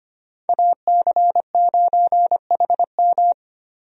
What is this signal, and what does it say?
Morse code, heard as a keyed tone, about 25 wpm, 700 Hz. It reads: AC9HM
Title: USA Call Signs - 5 Characters - 25wpm